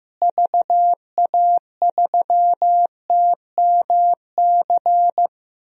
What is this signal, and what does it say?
Morse code, heard as a keyed tone, about 15 wpm, 700 Hz. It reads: VA3TMC
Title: DXCC-100 Call Signs - 6 Characters - Repeated 2x - 15wpm